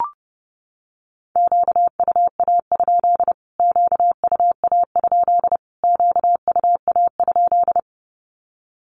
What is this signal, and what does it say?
Morse code, heard as a keyed tone, about 30 wpm, 700 Hz. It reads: QUA? QUA? QUA?
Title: Q-Codes Repeated 3x 30wpm